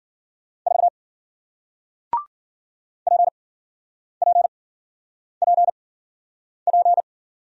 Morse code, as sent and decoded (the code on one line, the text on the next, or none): ....- / .--. .--. .--. .--.
4 PPPP